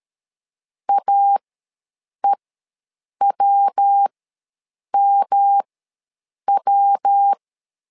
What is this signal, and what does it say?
Morse code, heard as a keyed tone, about 13 wpm, spaced as for 7 wpm, 800 Hz. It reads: AEWMW
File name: LEZIONI / 3 MWRO / 3 MW ws 12wpm 1816